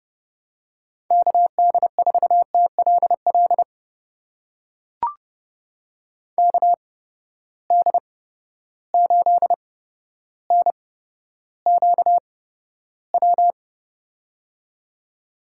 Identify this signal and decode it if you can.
Morse code, heard as a keyed tone, about 30 wpm, spaced as for 13 wpm, 700 Hz. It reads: KD4TLL KD8NQW
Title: USA Call Signs - 6 Characters - 8x Character Spacing - 30wpm